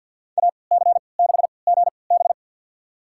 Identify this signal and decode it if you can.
Morse code, heard as a keyed tone, about 53 wpm, spaced as for 29 wpm, 700 Hz. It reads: AC6GB
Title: USA Call Signs - 5 Characters - 3x Character Spacing - 50wpm